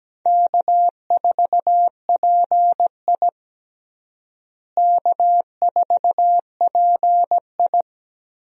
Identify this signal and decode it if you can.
Morse code, heard as a keyed tone, about 17 wpm, 700 Hz. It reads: K4PI K4PI
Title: USA Call Signs - 4 Characters - Repeated 2x - 17wpm